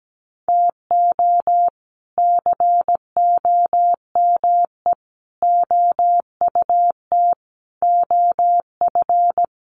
Morse code, heard as a keyed tone, about 17 wpm, 700 Hz. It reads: TO COME OUT OF
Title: Sets of 5 Words Encore2 17wpm